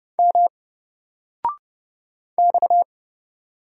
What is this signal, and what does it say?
Morse code, heard as a keyed tone, about 31 wpm, 700 Hz. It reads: M X
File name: Single-Letter-Number-30wpm